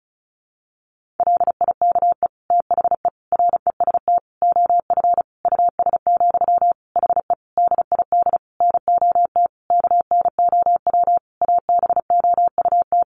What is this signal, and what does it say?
Morse code, heard as a keyed tone, about 34 wpm, 700 Hz. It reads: LIKE THE REST OF US, HE DID NOT KNOW ABOUT
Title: Sentences from Top 1400 Words 35wpm